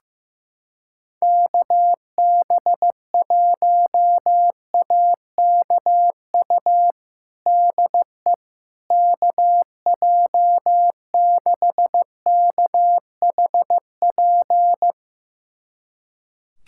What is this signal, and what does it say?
Morse code, heard as a keyed tone, about 16 wpm, 700 Hz. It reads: KB1AKU DE KJ6KHP
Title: USA Callsign DE Callsign - 6 Characters - 15wpm